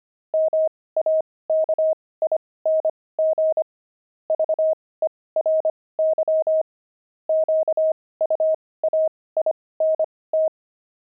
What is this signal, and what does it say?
Morse code, heard as a keyed tone, about 25 wpm, spaced as for 18 wpm, 630 Hz. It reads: MAKING VERY QUAINT
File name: pangram-fs-18-cs-25-toys